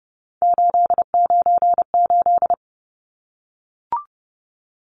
Morse code, heard as a keyed tone, about 30 wpm, 700 Hz. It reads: 898
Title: Numbers - 3-Digit 30wpm